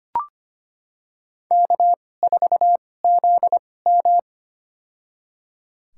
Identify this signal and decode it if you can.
Morse code, heard as a keyed tone, about 25 wpm, spaced as for 18 wpm, 700 Hz. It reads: K4ZM
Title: USA Call Signs - 4 Characters - 2x Character Spacing - 25wpm